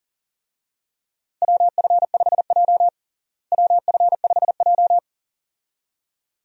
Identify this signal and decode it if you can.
Morse code, heard as a keyed tone, about 40 wpm, 700 Hz. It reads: WF5J WF5J
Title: USA Call Signs - 4 Characters - Repeated 2x - 40wpm